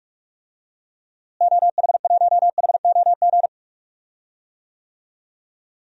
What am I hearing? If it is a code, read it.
Morse code, heard as a keyed tone, about 46 wpm, 700 Hz. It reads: OH1HOG
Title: DXCC-100 Call Signs - 6 Characters - Repeated 2x - 45wpm